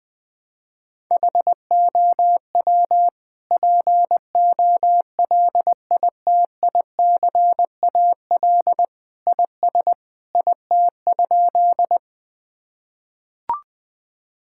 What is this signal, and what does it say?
Morse code, heard as a keyed tone, about 20 wpm, 700 Hz. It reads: HOW POLITICAL IS IT?
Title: Sentences from Top 1000 Words - Review Part 5 - 20wpm